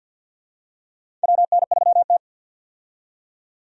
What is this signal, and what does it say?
Morse code, heard as a keyed tone, about 52 wpm, 700 Hz. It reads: WN3T
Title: USA Call Signs - 4 Characters - 50wpm